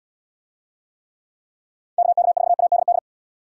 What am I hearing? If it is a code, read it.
Morse code, heard as a keyed tone, about 70 wpm, 700 Hz. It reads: KK4ADD